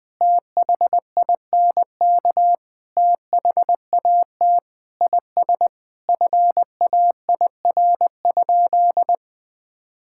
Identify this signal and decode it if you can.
Morse code, heard as a keyed tone, about 20 wpm, 700 Hz. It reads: THINK THAT IS FAIR?
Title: Sentences with 4-Word Phrases - Spoken Beginning - 20wpm